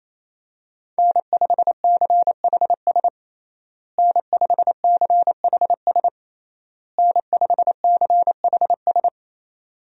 Morse code, heard as a keyed tone, about 27 wpm, 700 Hz. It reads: N5CHS N5CHS N5CHS
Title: USA Call Signs - 5 Characters - Repeated 3x - 28wpm